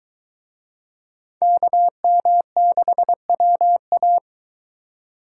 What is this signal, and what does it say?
Morse code, heard as a keyed tone, about 22 wpm, 700 Hz. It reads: KM6WA